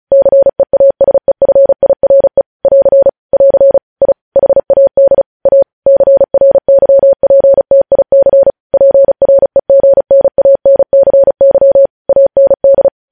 Morse code, heard as a keyed tone, about 36 wpm, 550 Hz. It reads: CEASEFIRE <AR> <AR> I HAD A CRYPTIC PREGNANCY AND
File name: podcast / News headlines-1x-35x35